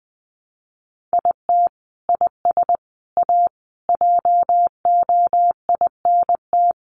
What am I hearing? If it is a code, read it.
Morse code, heard as a keyed tone, about 20 wpm, 700 Hz. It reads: IT IS A JOINT
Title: Sentences from Top 1300 Words 20wpm